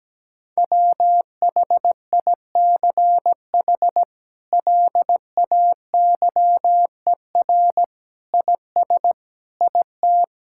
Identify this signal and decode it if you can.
Morse code, heard as a keyed tone, about 17 wpm, 700 Hz. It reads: WHICH LAYER IS IT
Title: Sentences from Top 1600 Words 17wpm